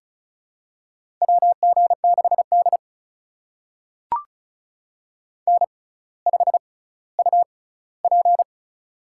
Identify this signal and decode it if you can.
Morse code, heard as a keyed tone, about 36 wpm, spaced as for 18 wpm, 700 Hz. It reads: WG6D N5UP
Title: USA Call Signs - 4 Characters - 6x Character Spacing - 35wpm